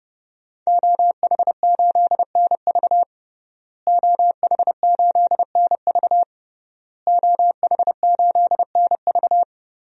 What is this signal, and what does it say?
Morse code, heard as a keyed tone, about 30 wpm, 700 Hz. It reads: OH8NV OH8NV OH8NV
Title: DXCC-100 Call Signs - 5 Characters - Repeated 3x - 30wpm